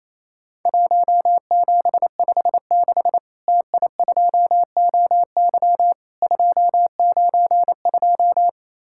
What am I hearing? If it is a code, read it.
Morse code, heard as a keyed tone, about 28 wpm, 700 Hz. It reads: 1756 TI2OY 292
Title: Worked All Europe DX Contest - QTC Exchange - 28wpm